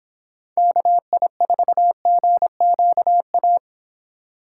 Morse code, heard as a keyed tone, about 25 wpm, 700 Hz. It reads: KI4GQA